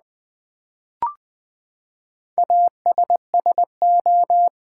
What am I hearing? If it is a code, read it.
Morse code, heard as a keyed tone, about 20 wpm, 700 Hz. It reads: ASSO
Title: Build A Word - 4 Letters At A Time - 11-Letter Words - 20wpm